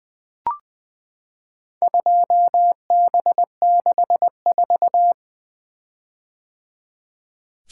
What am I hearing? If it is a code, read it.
Morse code, heard as a keyed tone, about 21 wpm, 700 Hz. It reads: 2B64
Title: Four Letter-Number Combinations - 20wpm